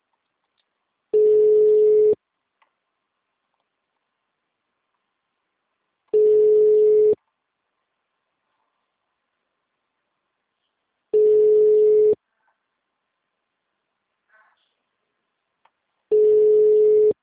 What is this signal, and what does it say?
Ringback tone (Europe, 425 Hz).